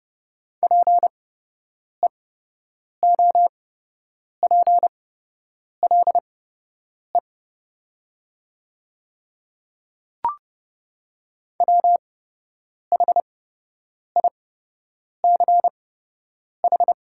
Morse code, heard as a keyed tone, about 30 wpm, spaced as for 8 wpm, 700 Hz. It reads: PEOPLE WHICH
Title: 8x Character Spacing - Top 500 Words - 30wpm